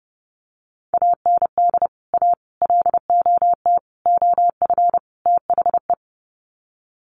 Morse code, heard as a keyed tone, about 30 wpm, 700 Hz. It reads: AND A LOT OF THE
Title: Sets of 5 Words Encore 30wpm